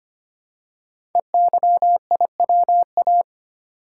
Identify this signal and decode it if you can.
Morse code, heard as a keyed tone, about 25 wpm, 700 Hz. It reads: EYIWA
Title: Code Groups - 5 Letters - 25wpm